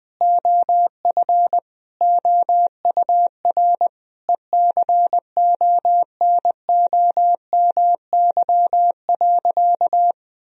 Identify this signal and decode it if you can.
Morse code, heard as a keyed tone, about 20 wpm, 700 Hz. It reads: OF OUR ECONOMY.
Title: Sentences from Top 1000 Words - Review Part 3 - 20wpm